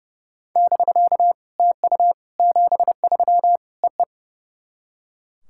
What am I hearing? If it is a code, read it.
Morse code, heard as a keyed tone, about 31 wpm, 700 Hz. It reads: <BK> TU 73 EE
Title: SOTA QSO Practice 30wpm